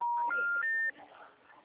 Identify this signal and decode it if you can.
Special information tone, with speech.